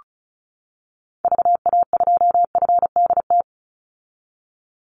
Morse code, heard as a keyed tone, about 34 wpm, 700 Hz. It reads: VA2FDT
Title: Call Signs - 35wpm